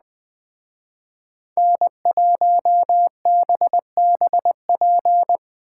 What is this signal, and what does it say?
Morse code, heard as a keyed tone, about 20 wpm, 700 Hz. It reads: N1BBP